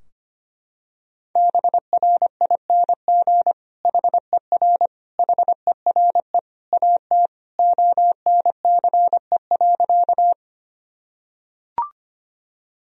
Morse code, heard as a keyed tone, about 25 wpm, 700 Hz. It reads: BRING HER HERE AT ONCE.
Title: Sentences from Top 1000 Words - Review Part 3 - 25wpm